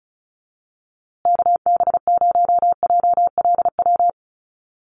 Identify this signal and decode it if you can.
Morse code, heard as a keyed tone, about 34 wpm, 700 Hz. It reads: KB0JLW
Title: USA Call Signs Repeated 2x 35wpm